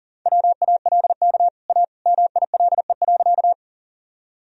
Morse code, heard as a keyed tone, about 37 wpm, 700 Hz. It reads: WALK A MILE.
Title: Sentences from Top 1000 Words - Review Part 1 - 40wpm